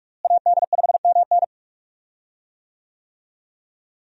Morse code, heard as a keyed tone, about 43 wpm, 700 Hz. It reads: AD5MN
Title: USA Call Signs - 5 Characters - Repeated 3x - 45wpm